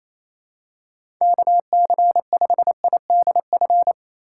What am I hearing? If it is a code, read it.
Morse code, heard as a keyed tone, about 27 wpm, 700 Hz. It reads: KC5IDF
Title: USA Call Signs - 6 Characters - 8x Character Spacing - 28wpm